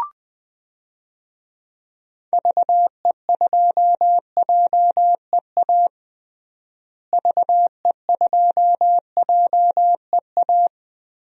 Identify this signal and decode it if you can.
Morse code, heard as a keyed tone, about 20 wpm, 700 Hz. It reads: VE2JEA VE2JEA
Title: DXCC-100 Call Signs - 6 Characters - Repeated 2x - 20wpm